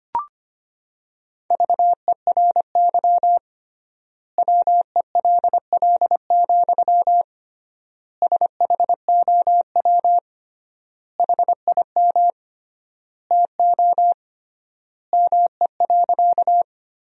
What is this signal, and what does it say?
Morse code, heard as a keyed tone, about 25 wpm, 700 Hz. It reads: VERY WELL, SHOW HIM TO ME.
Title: 3x Word-Spacing - Sentences from Top 200 Words - 25wpm